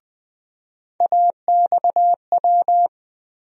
Morse code, heard as a keyed tone, about 20 wpm, 700 Hz. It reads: AXW
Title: USA Call Sign Postfixes - 3 Letters 20wpm